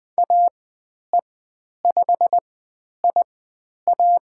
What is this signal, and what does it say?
Morse code, heard as a keyed tone, about 20 wpm, spaced as for 10 wpm, 700 Hz. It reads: AE5IA
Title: 102 Lesson 17g - Letter u - Call Signs 20wpm